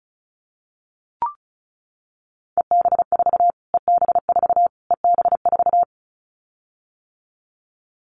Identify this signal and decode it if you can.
Morse code, heard as a keyed tone, about 34 wpm, 700 Hz. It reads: EB4 EB4 EB4
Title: Callsign Prefixes for Top 25 DXCC - Repeated 3x - 35wpm